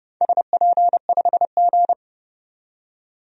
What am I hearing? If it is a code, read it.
Morse code, heard as a keyed tone, about 30 wpm, 700 Hz. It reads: SP5G
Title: DXCC-100 Call Signs - 4 Characters - 30wpm